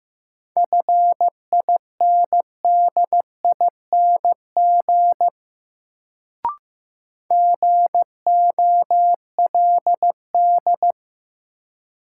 Morse code, heard as a keyed tone, about 15 wpm, 700 Hz. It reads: FINDING GOLD
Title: Sentences from Top 1100 Words - New Word Drill Alt - 15wpm